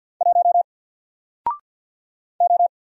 Morse code, heard as a keyed tone, about 48 wpm, 700 Hz. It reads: 1 O
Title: Single Letter-Number Alt 50wpm